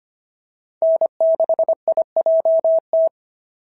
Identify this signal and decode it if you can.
Morse code, heard as a keyed tone, about 25 wpm, 690 Hz. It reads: N6IJT
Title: SOTA QSO Practice 25wpm